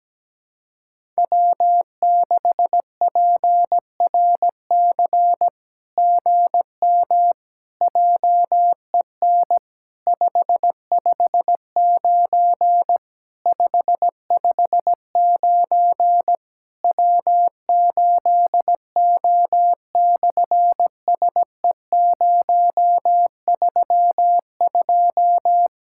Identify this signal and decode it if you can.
Morse code, heard as a keyed tone, about 17 wpm, 700 Hz. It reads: W6PRC GM JEN 559 559 W8O/SE032
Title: SOTA QSO Practice 17wpm